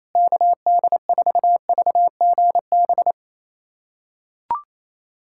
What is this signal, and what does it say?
Morse code, heard as a keyed tone, about 28 wpm, 700 Hz. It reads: KD4VGB
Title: USA Callsign DE Callsign 28wpm